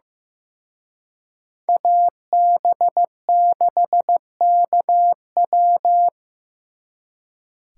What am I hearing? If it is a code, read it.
Morse code, heard as a keyed tone, about 19 wpm, 700 Hz. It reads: AB6KW
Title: Speed Racing - USA 2x2 Call Signs 15wpm